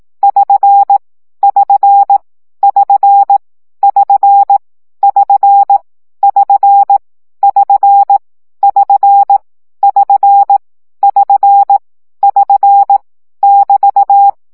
Morse code, heard as a keyed tone, about 18 wpm, 800 Hz. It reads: <SN> <SN> <SN> <SN> <SN> <SN> <SN> <SN> <SN> <SN> <SN> <BT>
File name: CW gyakorlas sn 15 baud